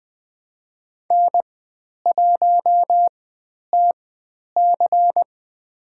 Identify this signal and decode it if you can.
Morse code, heard as a keyed tone, about 20 wpm, spaced as for 10 wpm, 700 Hz. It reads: N1TC
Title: 095 Lesson 16g - Letter c - Call Signs 20wpm